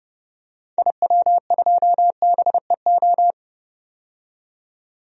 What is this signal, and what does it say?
Morse code, heard as a keyed tone, about 31 wpm, 700 Hz. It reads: IW2BEO
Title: DXCC-100 Call Signs - 6 Characters - 30wpm